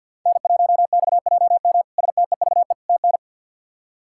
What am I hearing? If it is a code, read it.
Morse code, heard as a keyed tone, about 46 wpm, 700 Hz. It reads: N1XJM STEVE TN
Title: K1USN SST QSO Practice - 50wpm